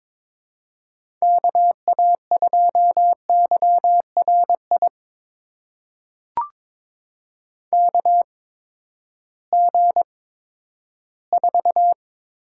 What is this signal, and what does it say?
Morse code, heard as a keyed tone, about 22 wpm, spaced as for 13 wpm, 700 Hz. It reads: KA2YRI KG4
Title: USA Call Signs - 6 Characters - 8x Character Spacing - 22wpm